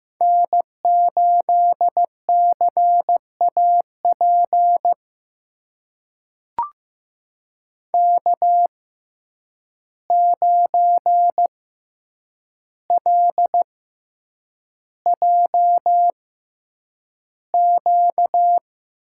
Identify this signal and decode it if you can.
Morse code, heard as a keyed tone, about 15 wpm, spaced as for 8 wpm, 700 Hz. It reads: N8CAP K9LJQ
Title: USA Call Signs - 5 Characters - 6x Character Spacing - 15wpm